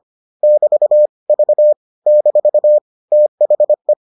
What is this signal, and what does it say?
Morse code, heard as a keyed tone, about 25 wpm, 600 Hz. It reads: <BT> V - THE